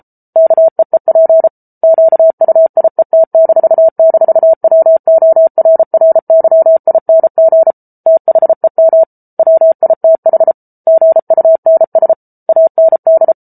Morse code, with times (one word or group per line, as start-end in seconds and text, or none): 0.4-1.5 KEEP
1.8-7.7 QUIET--WORRYING
8.1-9.0 THEM
9.4-10.5 WITH
10.9-12.1 GUNS
12.5-13.3 AND